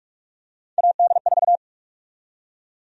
Morse code, heard as a keyed tone, about 42 wpm, 700 Hz. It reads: AD4
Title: USA Call Sign Prefixes - Repeated 2x - 45wpm